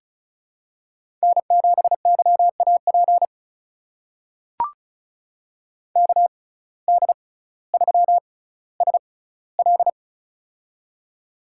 Morse code, heard as a keyed tone, about 36 wpm, spaced as for 18 wpm, 700 Hz. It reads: N7YAP KD3SL
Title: USA Call Signs - 5 Characters - 6x Character Spacing - 35wpm